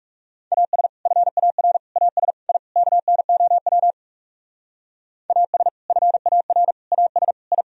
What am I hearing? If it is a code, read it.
Morse code, heard as a keyed tone, about 36 wpm, 700 Hz. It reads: AS FAR AS I KNOW AS FAR AS I
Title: Speed-Racing - Sets of 5 Words 30wpm